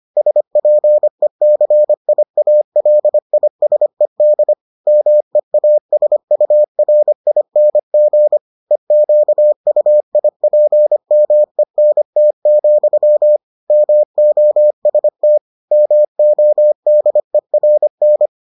Morse code, heard as a keyed tone, about 25 wpm, 600 Hz. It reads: SPECIALISED MEASURING EQUIPMENT, MOST MODERN